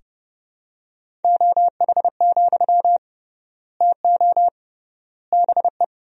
Morse code, heard as a keyed tone, about 30 wpm, 700 Hz. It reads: OH, TO BE
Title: 3x Word-Spacing - Sentences from Top 500 Words - 30wpm